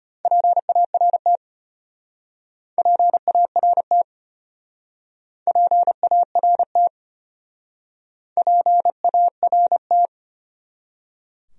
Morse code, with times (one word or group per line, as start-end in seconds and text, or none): 0.2-1.4 PART
2.8-4.0 PART
5.5-6.9 PART
8.4-10.1 PART